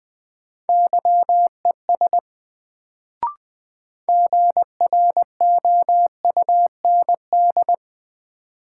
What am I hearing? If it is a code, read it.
Morse code, heard as a keyed tone, about 20 wpm, 700 Hz. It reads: YES GROUND
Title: Sentences from Top 500 Words - New Word Drill - 20wpm